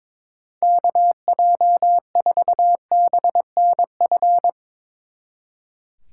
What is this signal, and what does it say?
Morse code, heard as a keyed tone, about 23 wpm, 700 Hz. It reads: KJ4BNF
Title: Speed Racing - USA 2x3 Call Signs 22wpm